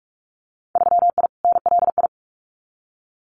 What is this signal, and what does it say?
Morse code, heard as a keyed tone, about 46 wpm, 700 Hz. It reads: 3I NLI
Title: Winter Field Day Exchanges 45wpm